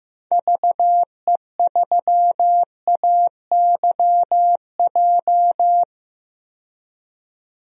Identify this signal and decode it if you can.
Morse code, heard as a keyed tone, about 15 wpm, 700 Hz. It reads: VE3AYJ
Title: DXCC-100 Call Signs - 6 Characters - 15wpm